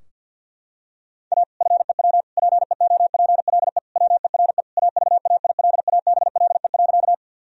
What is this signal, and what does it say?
Morse code, heard as a keyed tone, about 52 wpm, 700 Hz. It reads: A FEW PEOPLE WERE AVAILABLE.
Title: Sentences from Top 1000 Words - Review Part 3 - 50wpm